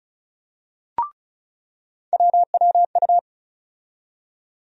Morse code, heard as a keyed tone, about 34 wpm, 700 Hz. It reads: WWU
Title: USA Call Sign Postfixes - 3 Letters 35wpm